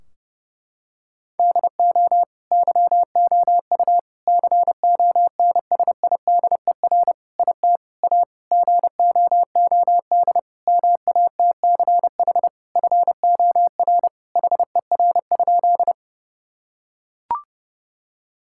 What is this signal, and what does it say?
Morse code, heard as a keyed tone, about 32 wpm, 700 Hz. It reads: DO YOU CONSIDER IT A GOOD MATCH FOR HER?
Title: Sentences from Top 1600 Words 30wpm